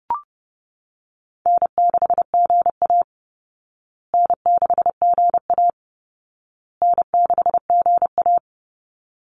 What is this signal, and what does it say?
Morse code, heard as a keyed tone, about 30 wpm, 700 Hz. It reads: N6GA N6GA N6GA
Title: Easy Call Signs Repeated 3x - 30wpm